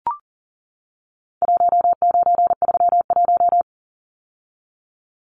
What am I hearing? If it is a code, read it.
Morse code, heard as a keyed tone, about 37 wpm, 700 Hz. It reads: 1931
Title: Numbers - 4-Digit 40wpm